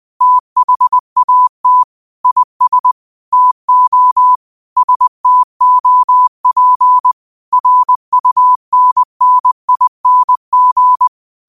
Morse code, heard as a keyed tone, about 20 wpm, 1000 Hz. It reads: THAT IS TO STOP RUNNING
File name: qotdfiles / 20wpm